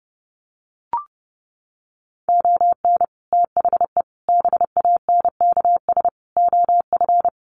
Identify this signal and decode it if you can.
Morse code, heard as a keyed tone, about 30 wpm, 700 Hz. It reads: ON THE BANKS OF
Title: Sets of 5 Words Encore2 30wpm